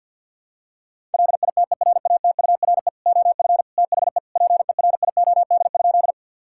Morse code, heard as a keyed tone, about 50 wpm, 700 Hz. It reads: LITERATURE OF THE PERIOD?
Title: Sentences from Top 1000 Words - Review Part 5 - 50wpm